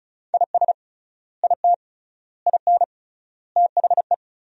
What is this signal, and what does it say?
Morse code, heard as a keyed tone, about 36 wpm, 700 Hz. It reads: IS IT IN THE
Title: 3x Word-Spacing - Sentences from Top 300 Words - 35wpm